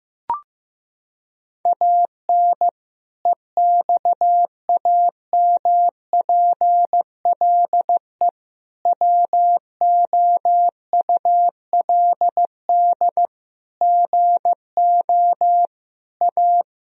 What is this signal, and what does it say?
Morse code, heard as a keyed tone, about 16 wpm, 700 Hz. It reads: AN EXAMPLE WOULD GO A
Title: Sentences from Top 1000 Words - Review Part 1 - 15wpm